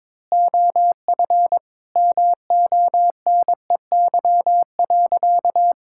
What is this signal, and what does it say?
Morse code, heard as a keyed tone, about 22 wpm, 700 Hz. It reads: OF MONEY.
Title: Sentences from Top 1000 Words - Review Part 6 - 22wpm